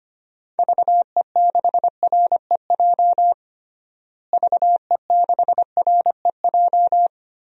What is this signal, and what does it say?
Morse code, heard as a keyed tone, about 25 wpm, 700 Hz. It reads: VE6REJ VE6REJ
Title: DXCC-100 Call Signs - 6 Characters - Repeated 3x - 25wpm